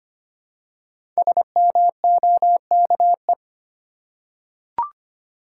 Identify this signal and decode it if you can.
Morse code, heard as a keyed tone, about 25 wpm, 700 Hz. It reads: SMOKE
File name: 5 Letter Words - First 3 Letters Given - 25wpm